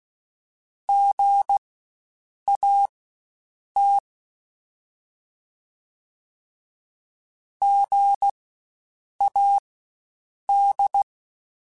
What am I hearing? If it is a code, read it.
Morse code, heard as a keyed tone, about 16 wpm, spaced as for 7 wpm, 800 Hz. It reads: GAT GAD
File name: Spaced Alphas / Spaced Alphas mp3s / Gs sp @15